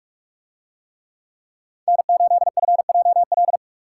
Morse code, heard as a keyed tone, about 44 wpm, 700 Hz. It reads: N8FJL